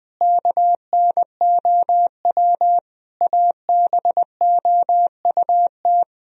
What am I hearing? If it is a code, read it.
Morse code, heard as a keyed tone, about 21 wpm, 700 Hz. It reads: KNOW ABOUT
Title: Sentences from Top 1000 Words - Review Part 4 - 20wpm